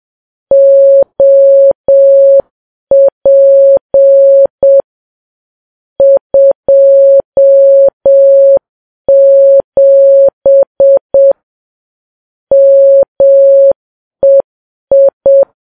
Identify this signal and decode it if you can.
Morse code, heard as a keyed tone, about 7 wpm, 550 Hz. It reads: OP 27 MEI
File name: Files Web tls / Morse on9fv / morse7kps